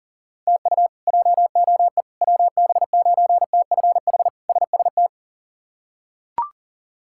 Morse code, heard as a keyed tone, about 40 wpm, 700 Hz. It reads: TU JOE WB9TFH SST
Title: K1USN SST QSO Practice - 40wpm